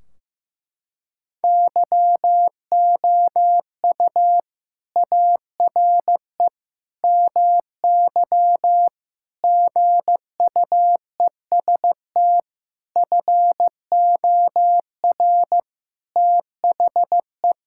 Morse code, heard as a keyed tone, about 15 wpm, 700 Hz. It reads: YOU ARE MY GUEST FOR THE
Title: Sentences from Top 1600 Words 15wpm